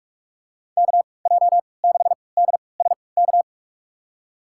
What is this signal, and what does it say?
Morse code, heard as a keyed tone, about 41 wpm, spaced as for 25 wpm, 700 Hz. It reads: KJ6DSK
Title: USA Call Signs - 6 Characters - 3x Character Spacing - 45wpm